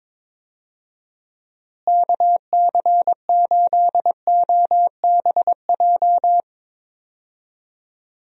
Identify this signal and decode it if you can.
Morse code, heard as a keyed tone, about 22 wpm, 700 Hz. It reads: KC8OBJ
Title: USA Call Signs - 6 Characters - Repeated 2x - 22wpm